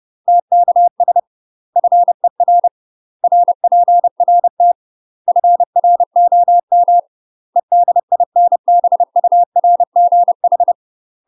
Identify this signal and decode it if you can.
Morse code, heard as a keyed tone, about 30 wpm, 700 Hz. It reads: TKS FER RPRT FROM EDINBURGH
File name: UK MASTER QSOs TRAINING / 30WPM / UK MASTER QSOs 0048